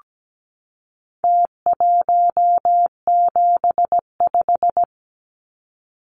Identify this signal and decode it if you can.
Morse code, heard as a keyed tone, about 17 wpm, 700 Hz. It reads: T175
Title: Numbers - Contest Style - 4-Digit 17wpm